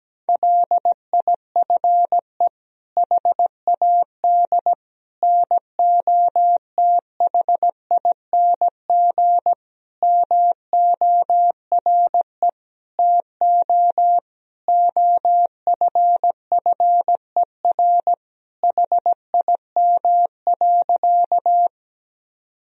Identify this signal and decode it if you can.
Morse code, heard as a keyed tone, about 17 wpm, 700 Hz. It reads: LIFE HAD NOTHING MORE TO OFFER HIM.
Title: Sentences from Top 1000 Words - Review Part 4 - 17wpm